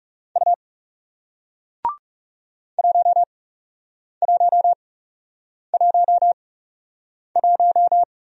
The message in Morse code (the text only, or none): U 1111